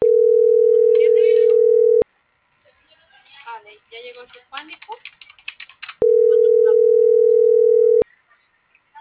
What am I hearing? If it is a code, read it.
Ringback tone (North America), 440 + 480 Hz.